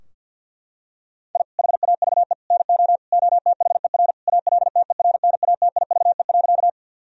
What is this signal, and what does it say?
Morse code, heard as a keyed tone, about 50 wpm, 700 Hz. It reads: I HAVE NO OTHER ALTERNATIVE.
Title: Sentences from Top 1000 Words - Review Part 4 - 50wpm